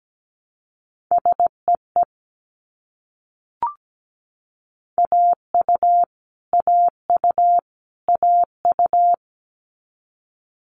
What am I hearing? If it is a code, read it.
Morse code, heard as a keyed tone, about 17 wpm, 700 Hz. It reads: SEE AU AU AU
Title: Multi-letter Phonemes Repeated 3x 17wpm